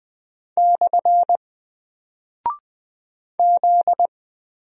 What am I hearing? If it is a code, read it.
Morse code, heard as a keyed tone, about 20 wpm, 700 Hz. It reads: / Z
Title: Single Character (Letter:Number:Punctuation:Prosign) 20wpm